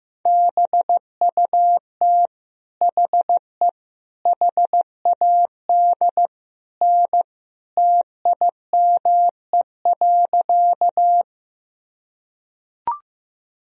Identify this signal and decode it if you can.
Morse code, heard as a keyed tone, about 16 wpm, 700 Hz. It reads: BUT HE HAD N TIME.
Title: Sentences from Top 100 Words and Abbreviations 15wpm